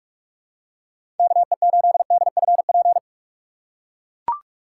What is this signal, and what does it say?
Morse code, heard as a keyed tone, about 44 wpm, 700 Hz. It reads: KE8DFP
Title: USA Call Signs - 6 Characters - 2x Character Spacing - 45wpm